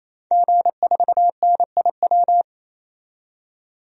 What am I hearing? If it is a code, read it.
Morse code, heard as a keyed tone, about 28 wpm, 700 Hz. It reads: G4NIW
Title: DXCC-100 Call Signs - 5 Characters - 28wpm